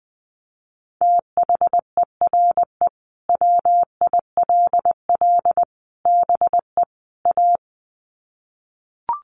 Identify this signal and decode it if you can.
Morse code, heard as a keyed tone, about 20 wpm, 700 Hz. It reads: THERE WILL BE A
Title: Sets of 4 Words 20wpm